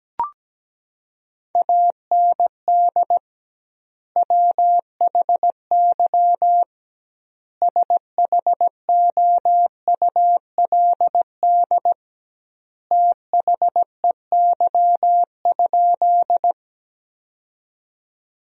Morse code, heard as a keyed tone, about 17 wpm, 700 Hz. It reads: AND WHY SHOULD THEY?
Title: 2x Word-Spacing - Sentences from Top 300 Words - 17wpm